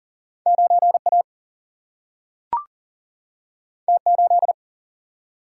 Morse code, heard as a keyed tone, about 40 wpm, 700 Hz. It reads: 9 A T 8
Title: Single-Letter Single-Number 40wpm